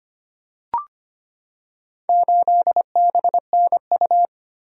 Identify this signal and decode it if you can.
Morse code, heard as a keyed tone, about 25 wpm, 700 Hz. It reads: 8BNU
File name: practice / 25wpm / Four Letter-Number Combinations - 25wpm